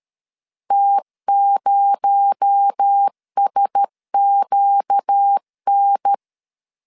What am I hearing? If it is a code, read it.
Morse code, heard as a keyed tone, about 13 wpm, 800 Hz. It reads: T0SQN
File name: CW / DET / 1 6 1500C S A 12wpm 3440